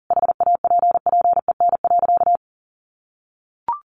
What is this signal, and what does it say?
Morse code, heard as a keyed tone, about 40 wpm, 700 Hz. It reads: HAPPEN.
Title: Sentences from Top 1200 Words 40wpm